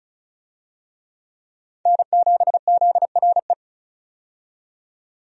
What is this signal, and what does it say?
Morse code, heard as a keyed tone, about 36 wpm, 700 Hz. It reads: N7ZRE
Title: USA Call Signs - 5 Characters - 35wpm